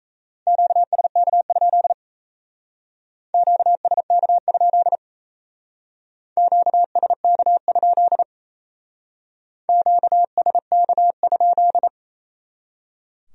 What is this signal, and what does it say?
Morse code, heard as a keyed tone, about 34 wpm, 700 Hz. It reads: QSK? QSK? QSK? QSK?
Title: Speed-Racing Q-Codes - 28wpm